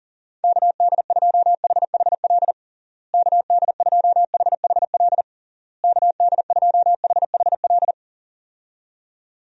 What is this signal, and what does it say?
Morse code, heard as a keyed tone, about 40 wpm, 700 Hz. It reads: KD2HHL KD2HHL KD2HHL
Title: USA Call Signs - 6 Characters - Repeated 3x - 40wpm